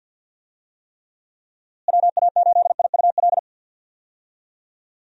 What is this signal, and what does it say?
Morse code, heard as a keyed tone, about 51 wpm, 700 Hz. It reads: WA8IUL
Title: USA Call Signs - 6 Characters - Repeated 3x - 50wpm